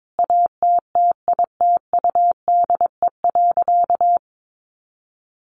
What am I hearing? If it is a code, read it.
Morse code, heard as a keyed tone, about 22 wpm, 700 Hz. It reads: ATTITUDE.
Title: Sentences from Top 1300 Words 22wpm